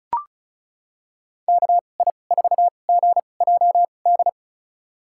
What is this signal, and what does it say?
Morse code, heard as a keyed tone, about 36 wpm, spaced as for 25 wpm, 700 Hz. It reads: KI4GJD
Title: USA Call Signs - 6 Characters - 2x Character Spacing - 35wpm